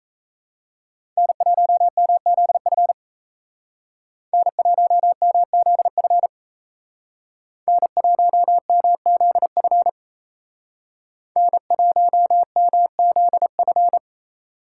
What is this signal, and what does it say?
Morse code, heard as a keyed tone, about 36 wpm, 700 Hz. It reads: N1MZF N1MZF N1MZF N1MZF